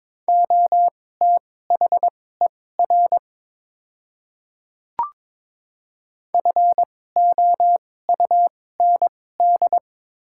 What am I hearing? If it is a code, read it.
Morse code, heard as a keyed tone, about 22 wpm, spaced as for 16 wpm, 690 Hz. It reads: OTHER FOUND
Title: 2x Character Spacing - Top 300 Words - 22wpm